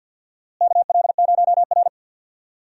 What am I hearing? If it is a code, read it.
Morse code, heard as a keyed tone, about 52 wpm, 700 Hz. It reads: KL0R
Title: USA Call Signs - 4 Characters - Repeated 2x - 50wpm